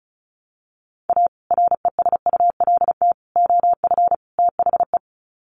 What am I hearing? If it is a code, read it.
Morse code, heard as a keyed tone, about 34 wpm, 700 Hz. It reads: A RESULT OF THE
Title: Sets of 4 Words 35wpm